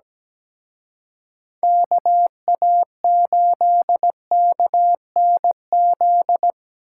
Morse code, heard as a keyed tone, about 20 wpm, 700 Hz. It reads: KA8KNZ
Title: Speed Racing - USA 2x3 Call Signs 17wpm